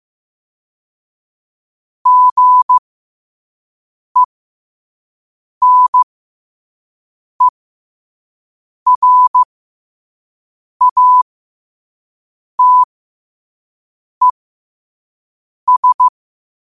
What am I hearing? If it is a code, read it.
Morse code, heard as a keyed tone, about 15 wpm, spaced as for 5 wpm, 1000 Hz. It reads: GENERATES
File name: qotdfiles / 5wpm